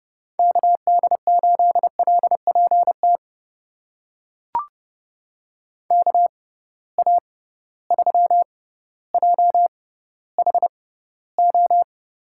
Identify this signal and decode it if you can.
Morse code, heard as a keyed tone, about 32 wpm, spaced as for 16 wpm, 700 Hz. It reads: KD8LPT KA3JHO
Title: USA Call Signs - 6 Characters - 6x Character Spacing - 30wpm